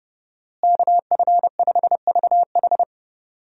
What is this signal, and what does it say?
Morse code, heard as a keyed tone, about 30 wpm, 700 Hz. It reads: KF5VH
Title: USA Call Signs - 5 Characters - Repeated 3x - 30wpm